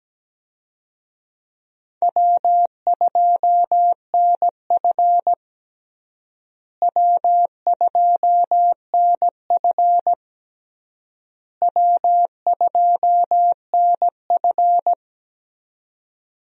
Morse code, heard as a keyed tone, about 17 wpm, 700 Hz. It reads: W2NF W2NF W2NF
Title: USA Call Signs - 4 Characters - Repeated 3x - 17wpm